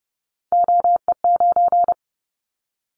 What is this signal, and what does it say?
Morse code, heard as a keyed tone, about 30 wpm, 700 Hz. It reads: OE9
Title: Callsign Prefixes for Top 100 DXCC - 30wpm